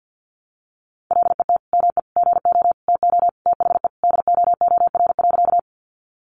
Morse code, heard as a keyed tone, about 50 wpm, 700 Hz. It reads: LET ME GO TO THE DOOR.
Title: Sentences from Top 300 Words 50wpm